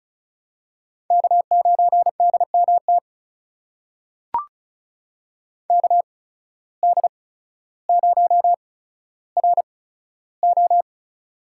Morse code, heard as a keyed tone, about 36 wpm, spaced as for 15 wpm, 700 Hz. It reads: K9DMT KD0RO
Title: USA Call Signs - 5 Characters - 8x Character Spacing - 35wpm